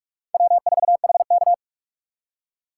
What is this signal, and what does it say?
Morse code, heard as a keyed tone, about 46 wpm, 700 Hz. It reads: W4HK
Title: USA Call Signs - 4 Characters - 4x Character Spacing - 45wpm